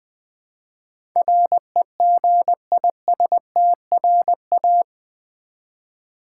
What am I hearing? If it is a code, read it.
Morse code, heard as a keyed tone, about 20 wpm, 700 Hz. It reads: REGISTRA
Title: Build A Word - 4 Letters At A Time - 12-Letter Words - 20wpm